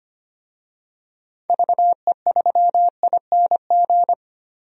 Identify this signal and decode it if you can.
Morse code, heard as a keyed tone, about 25 wpm, 700 Hz. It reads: VE3ING
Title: DXCC-100 Call Signs - 6 Characters - Repeated 3x - 25wpm